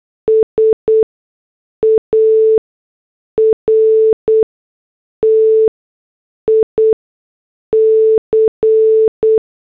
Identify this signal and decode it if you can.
Morse code, heard as a keyed tone, about 8 wpm, spaced as for 6 wpm, 440 Hz. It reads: SARTIC